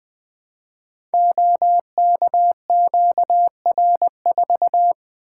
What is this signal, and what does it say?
Morse code, heard as a keyed tone, about 20 wpm, 700 Hz. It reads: OKQR4
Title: Code Groups - 5 Letter:Numbers - 20wpm